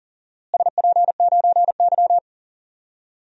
Morse code, heard as a keyed tone, about 40 wpm, 700 Hz. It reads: SP9Y
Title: DXCC-100 Call Signs - 4 Characters - Repeated 2x - 40wpm